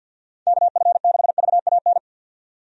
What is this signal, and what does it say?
Morse code, heard as a keyed tone, about 52 wpm, 700 Hz. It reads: KF6VAN
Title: USA Call Signs - 6 Characters - 50wpm